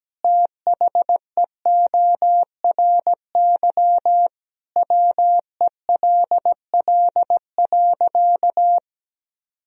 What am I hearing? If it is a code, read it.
Morse code, heard as a keyed tone, about 17 wpm, 700 Hz. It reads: THEORY WELL.
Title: Sentences from Top 1000 Words - Review Part 3 - 17wpm